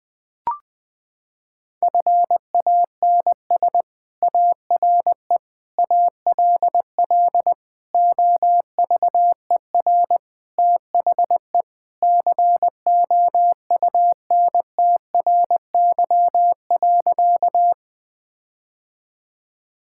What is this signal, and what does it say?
Morse code, heard as a keyed tone, about 20 wpm, 700 Hz. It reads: FANS ARE ALL OVER THE COUNTRY.
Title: Sentences with 4-Word Phrases - 20wpm